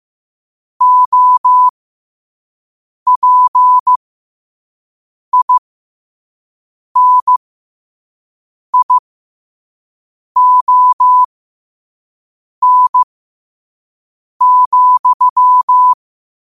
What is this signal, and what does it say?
Morse code, heard as a keyed tone, about 15 wpm, spaced as for 5 wpm, 1000 Hz. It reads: OPINION,
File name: qotdfiles / 5wpm